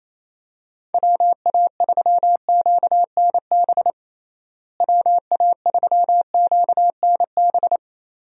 Morse code, heard as a keyed tone, about 27 wpm, 700 Hz. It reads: WA3QNB WA3QNB
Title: USA Call Signs - 6 Characters - Repeated 2x - 28wpm